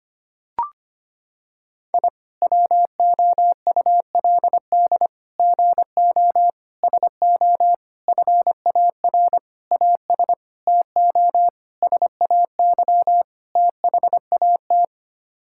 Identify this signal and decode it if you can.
Morse code, heard as a keyed tone, about 25 wpm, 700 Hz. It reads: I WOULD GO SO FAR AS TO SAY THAT
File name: Sets of 9 Words - 25wpm